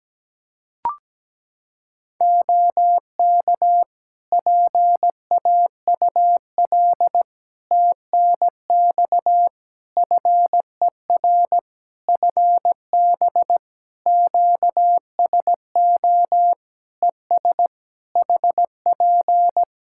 Morse code, heard as a keyed tone, about 17 wpm, 700 Hz. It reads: OK PAUL TNX FER FB QSO ES HP
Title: K1USN SST QSO Practice 17wpm